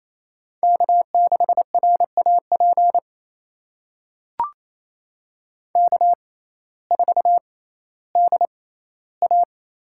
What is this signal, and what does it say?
Morse code, heard as a keyed tone, about 28 wpm, spaced as for 15 wpm, 700 Hz. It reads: K6RAP K4DA
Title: USA Call Signs - 5 Characters - 6x Character Spacing - 28wpm